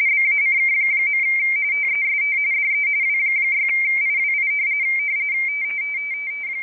RTTY, about 45.45 baud. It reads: BACK. TEST DE W6IWI
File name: rtty / websdr recording 14086.0kHz